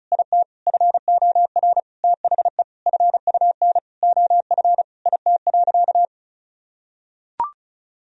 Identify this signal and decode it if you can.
Morse code, heard as a keyed tone, about 36 wpm, 700 Hz. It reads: IT FOR THE FUN OF IT.
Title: Sentences from Top 1000 Words - Review Part 4 - 35wpm